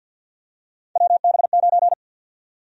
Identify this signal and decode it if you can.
Morse code, heard as a keyed tone, about 52 wpm, 700 Hz. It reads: WB9
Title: USA Call Sign Prefixes - 50wpm